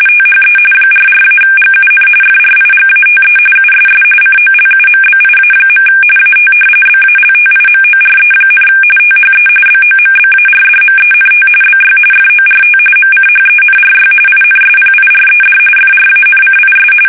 RTTY, about 45.45 baud. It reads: THE 44-GUN FRIGATE USS COLUMBIA. AT BALTIMORE, THE BOMBARDMENT BY FORT MCHENRY INSPIRED FRANCIS
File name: fsk850 / usnavy2